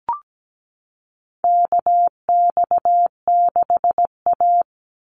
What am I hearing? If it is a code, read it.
Morse code, heard as a keyed tone, about 17 wpm, 700 Hz. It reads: KX6A
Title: NA SOTA Chaser Callsigns 17wpm